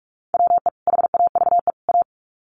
Morse code, heard as a keyed tone, about 44 wpm, 700 Hz. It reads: WE HAVE A
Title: Sets of 3 Words 45wpm